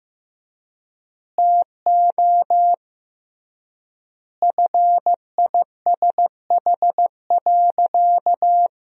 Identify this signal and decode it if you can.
Morse code, heard as a keyed tone, about 16 wpm, 700 Hz. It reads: TO FISH.
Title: 3x Word-Spacing - Sentences from Top 300 Words - 15wpm